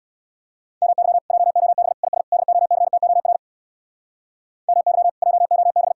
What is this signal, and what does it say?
Morse code, heard as a keyed tone, about 63 wpm, 700 Hz. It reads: MY JOB IS DOCTOR MY JOB